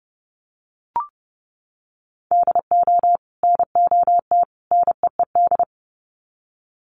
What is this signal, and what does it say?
Morse code, heard as a keyed tone, about 30 wpm, 700 Hz. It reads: DO NOT NEED
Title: Sets of 3 Words 30wpm